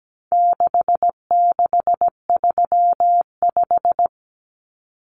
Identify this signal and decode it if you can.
Morse code, heard as a keyed tone, about 17 wpm, 700 Hz. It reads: 6635
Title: Numbers - 4-Digit 17wpm